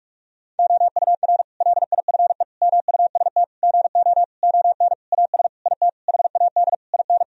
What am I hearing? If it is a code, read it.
Morse code, heard as a keyed tone, about 41 wpm, 700 Hz. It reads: OUR LIFE MUST GO ON AS IT HAD IN
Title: Sentences from Top 1000 Words - Review Part 3 - 45wpm